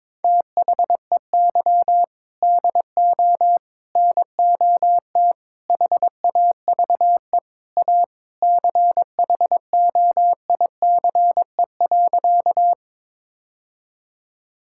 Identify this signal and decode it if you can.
Morse code, heard as a keyed tone, about 22 wpm, 700 Hz. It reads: THEY DO NOT HAVE A CHOICE.
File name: practice / 22wpm / Sentences with 4-Word Phrases - 22wpm